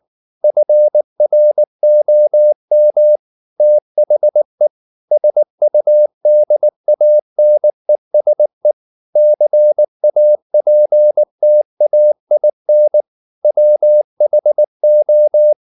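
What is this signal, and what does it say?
Morse code, heard as a keyed tone, about 19 wpm, 600 Hz. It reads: FROM THE SUDANESE CAPTAIN WHO
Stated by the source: Morse code at 19 wpm from CW med Gnister, reading Ken Follett's novel Never in English.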